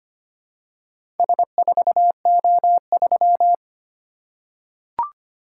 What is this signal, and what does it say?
Morse code, heard as a keyed tone, about 25 wpm, 700 Hz. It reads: S4O3
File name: practice / 25wpm / Four Letter-Number Combinations - 25wpm